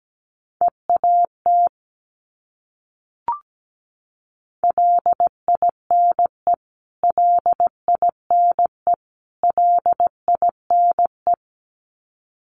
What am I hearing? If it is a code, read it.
Morse code, heard as a keyed tone, about 17 wpm, 700 Hz. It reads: EAT LINE LINE LINE
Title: Top 500 Words Repeated 17wpm